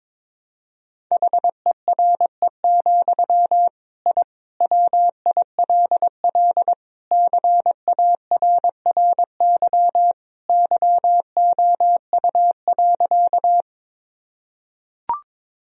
Morse code, heard as a keyed tone, about 21 wpm, 700 Hz. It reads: HERE, I WILL CARRY YOU.
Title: Sentences from Top 1000 Words - Review Part 1 - 22wpm